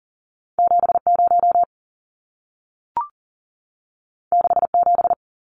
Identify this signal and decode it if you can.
Morse code, heard as a keyed tone, about 40 wpm, 700 Hz. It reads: 70 67
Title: Numbers - 2-Digit 40wpm